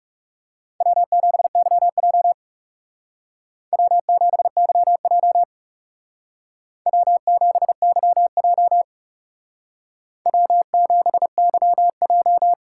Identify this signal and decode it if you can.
Morse code, heard as a keyed tone, about 37 wpm, 700 Hz. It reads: W7YJ W7YJ W7YJ W7YJ
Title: Speed Racing - USA 1x2 Call Signs 30wpm